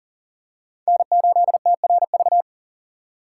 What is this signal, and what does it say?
Morse code, heard as a keyed tone, about 40 wpm, 690 Hz. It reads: N8TRV
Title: USA Call Signs - 6x Character Spacing - 40wpm